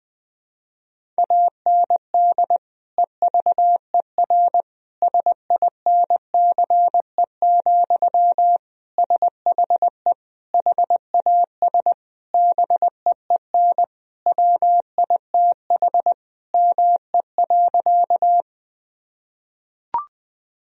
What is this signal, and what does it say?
Morse code, heard as a keyed tone, about 21 wpm, 700 Hz. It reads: AND EVER SINCE, SHE HAS BEEN WITH ME.
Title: Sentences from Top 1000 Words - Review Part 1 - 20wpm